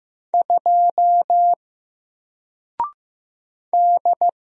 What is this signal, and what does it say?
Morse code, heard as a keyed tone, about 15 wpm, 700 Hz. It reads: 2 D
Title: Single Letter-Number Alt 15wpm